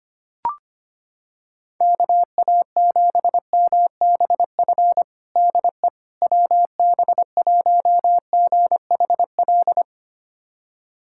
Morse code, heard as a keyed tone, about 25 wpm, 700 Hz. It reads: KA7MBF DE WB1GHL
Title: USA Callsign DE Callsign - 6 Characters - 25wpm